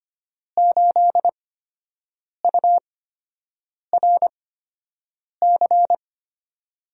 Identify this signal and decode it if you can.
Morse code, heard as a keyed tone, about 25 wpm, spaced as for 7 wpm, 700 Hz. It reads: 8URC
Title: USA Call Signs - 6 Characters - 8x Character Spacing - 25wpm